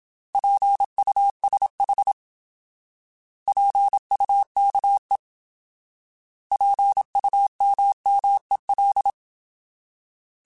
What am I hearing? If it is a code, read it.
Morse code, heard as a keyed tone, about 27 wpm, 800 Hz. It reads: PUSH PUKE PUMMEL
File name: Alphabetical / Alphabetical mp3s / Ps @25